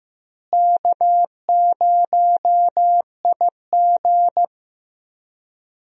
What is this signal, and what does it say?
Morse code, heard as a keyed tone, about 15 wpm, 700 Hz. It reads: K0IG
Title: USA Call Signs - 4 Characters - Repeated 2x - 15wpm